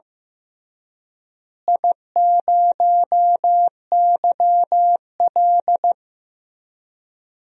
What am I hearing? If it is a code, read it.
Morse code, heard as a keyed tone, about 15 wpm, 700 Hz. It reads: I0YL
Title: DXCC-100 Call Signs - 4 Characters - Repeated 3x - 15wpm